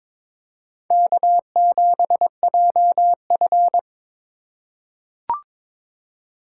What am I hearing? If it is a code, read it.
Morse code, heard as a keyed tone, about 22 wpm, 700 Hz. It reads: K7JF
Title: USA Call Signs - 2x Character Spacing - 22wpm